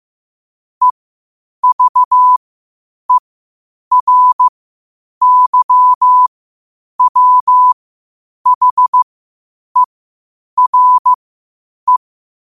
Morse code, heard as a keyed tone, about 15 wpm, spaced as for 8 wpm, 1000 Hz. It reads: EVERYWHERE
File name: qotdfiles / 8wpm